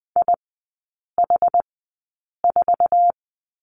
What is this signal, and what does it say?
Morse code, heard as a keyed tone, about 20 wpm, spaced as for 8 wpm, 700 Hz. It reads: IH4
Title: 067 Lesson 12e - Letter d - Three Characters 20wpm